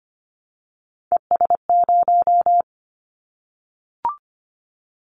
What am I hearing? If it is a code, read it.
Morse code, heard as a keyed tone, about 25 wpm, 700 Hz. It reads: ES0
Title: Callsign Prefixes for Top 100 DXCC - 25wpm